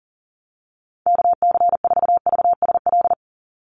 Morse code, heard as a keyed tone, about 40 wpm, 690 Hz. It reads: KC4VSL
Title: USA Call Signs 40wpm